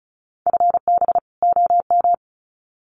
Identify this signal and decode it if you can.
Morse code, heard as a keyed tone, about 34 wpm, 700 Hz. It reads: FB OM
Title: Common QSO Phrases 35wpm